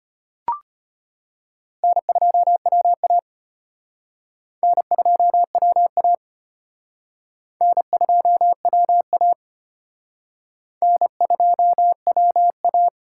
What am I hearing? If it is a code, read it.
Morse code, heard as a keyed tone, about 31 wpm, 700 Hz. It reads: N2WA N2WA N2WA N2WA